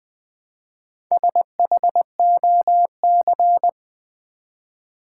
Morse code, heard as a keyed tone, about 20 wpm, 700 Hz. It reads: SHOC
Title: Build A Word - 1 Letter At A Time - 5-Letter Words - 20wpm